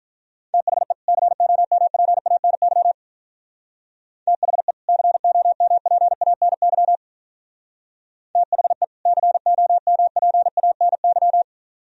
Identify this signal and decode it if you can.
Morse code, heard as a keyed tone, about 43 wpm, 700 Hz. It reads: THE COMPANY THE COMPANY THE COMPANY
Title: Speed-Racing - Sets of 2 Words 35wpm